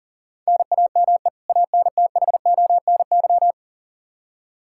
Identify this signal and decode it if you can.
Morse code, heard as a keyed tone, about 39 wpm, 700 Hz. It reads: NAME ANTHONY
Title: K1USN SST QSO Practice 40wpm